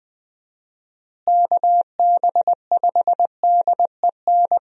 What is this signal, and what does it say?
Morse code, heard as a keyed tone, about 20 wpm, 700 Hz. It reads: KB5DEN
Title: USA Call Signs - 6 Characters - Repeated 2x - 20wpm